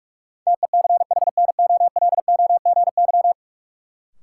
Morse code, heard as a keyed tone, about 46 wpm, 700 Hz. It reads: TECHNOLOGY
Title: Sentences from Top 1200 Words - New Word Drill Alt - 45wpm